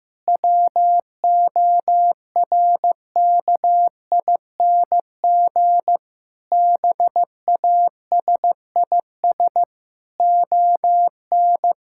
Morse code, heard as a keyed tone, about 15 wpm, 700 Hz. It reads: WORKING BASIS ON
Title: Sentences from Top 1000 Words - Review Part 5 - 15wpm